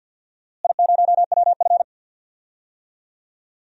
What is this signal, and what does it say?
Morse code, heard as a keyed tone, about 50 wpm, 700 Hz. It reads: I0WF
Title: DXCC-100 Call Signs - 4 Characters - Repeated 2x - 50wpm